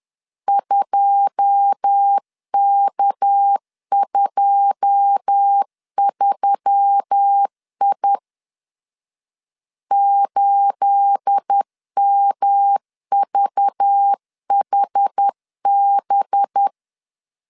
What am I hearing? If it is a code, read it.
Morse code, heard as a keyed tone, about 10 wpm, 800 Hz. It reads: 2K23I 8MVHB
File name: CW / DET / 1 9 1500C S B 10wpm 4439